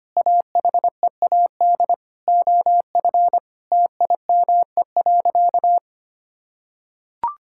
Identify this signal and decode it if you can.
Morse code, heard as a keyed tone, about 25 wpm, 700 Hz. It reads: AHEAD OF TIME.
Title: Sentences from Top 1000 Words - Review Part 6 - 25wpm